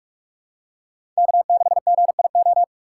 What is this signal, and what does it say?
Morse code, heard as a keyed tone, about 44 wpm, 690 Hz. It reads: K6GIO
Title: USA Call Signs - 5 Characters - 6x Character Spacing - 45wpm